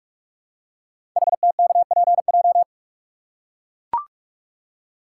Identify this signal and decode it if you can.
Morse code, heard as a keyed tone, about 44 wpm, 700 Hz. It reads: HTKPJ
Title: Code Groups - 5 Letter:Numbers - 45wpm